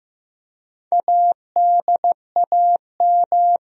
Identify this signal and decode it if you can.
Morse code, heard as a keyed tone, about 15 wpm, 700 Hz. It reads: ADAM
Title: USA Common Names - 15wpm